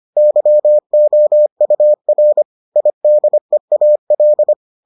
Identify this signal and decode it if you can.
Morse code, heard as a keyed tone, about 25 wpm, 600 Hz. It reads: YOUR IDEAL